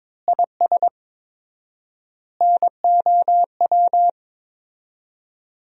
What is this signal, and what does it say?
Morse code, heard as a keyed tone, about 22 wpm, 700 Hz. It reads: IS NOW
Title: 4x Word-Spacing - Sentences from Top 400 Words - 22wpm